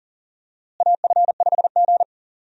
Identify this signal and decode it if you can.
Morse code, heard as a keyed tone, about 40 wpm, 700 Hz. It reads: AF5G
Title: USA Call Signs - 4 Characters - Repeated 2x - 40wpm